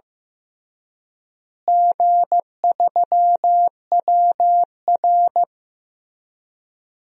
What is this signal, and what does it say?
Morse code, heard as a keyed tone, about 15 wpm, 700 Hz. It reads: G3WR
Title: DXCC-100 Call Signs - 4 Characters - Repeated 3x - 15wpm